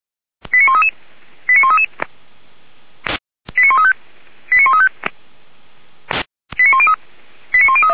Five-tone selective call (ZVEI), heard as quick runs of five tones, tones of about 70 ms. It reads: 89133 89133 89135 89135 89103
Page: ZVEI